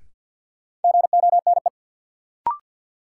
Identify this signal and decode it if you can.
Morse code, heard as a keyed tone, about 53 wpm, 700 Hz. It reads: GONE
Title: Sentences from Top 1200 Words - New Word Drill - 50wpm